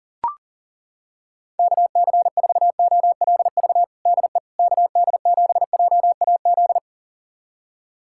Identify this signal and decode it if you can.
Morse code, heard as a keyed tone, about 40 wpm, 700 Hz. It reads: KC4OLV DE KD7JAZ
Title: USA Callsign DE Callsign - 6 Characters - 40wpm